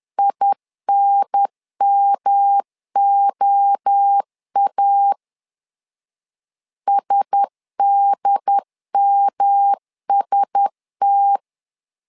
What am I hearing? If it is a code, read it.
Morse code, heard as a keyed tone, about 10 wpm, 800 Hz. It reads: INMOA SDMST
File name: LEZIONI / 4 Q1S 10wpm 2133